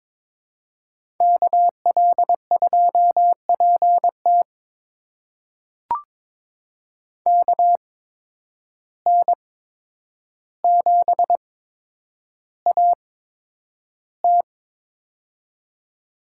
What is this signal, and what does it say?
Morse code, heard as a keyed tone, about 22 wpm, spaced as for 9 wpm, 700 Hz. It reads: KL2PT KN7AT
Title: USA Call Signs - 5 Characters - 8x Character Spacing - 22wpm